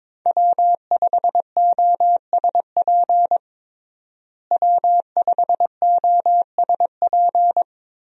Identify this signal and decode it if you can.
Morse code, heard as a keyed tone, about 22 wpm, 700 Hz. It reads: W5OSP W5OSP
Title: USA Call Signs - 5 Characters - Repeated 3x - 22wpm